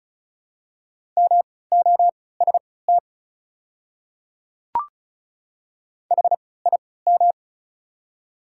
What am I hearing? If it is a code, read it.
Morse code, heard as a keyed tone, about 36 wpm, spaced as for 20 wpm, 690 Hz. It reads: MOST HIM
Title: 3x Character Spacing - Top 100 Words - 35wpm